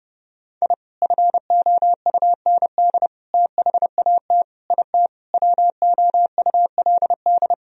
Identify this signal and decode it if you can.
Morse code, heard as a keyed tone, about 30 wpm, 700 Hz. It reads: I FOUND THAT IT WOULD
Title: Sentences from Top 1000 Words - Review Part 1 - 30wpm